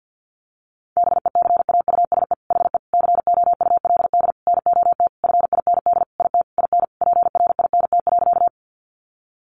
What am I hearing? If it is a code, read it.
Morse code, heard as a keyed tone, about 50 wpm, 700 Hz. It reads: BECAUSE HE COULD NOT FIND IT IN PRINT.
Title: Sentences from Top 1400 Words 50wpm